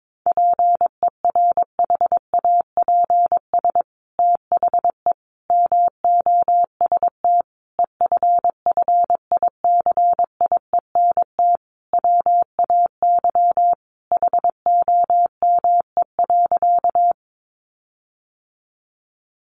Morse code, heard as a keyed tone, about 22 wpm, 700 Hz. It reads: PERHAPS THE MOST EFFICIENT WAY HOME.
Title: Sentences from Top 1300 Words 22wpm